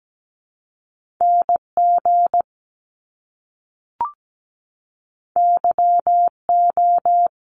Morse code, heard as a keyed tone, about 17 wpm, 700 Hz. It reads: NG YO
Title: Top Two Letter Combinations in Words 17wpm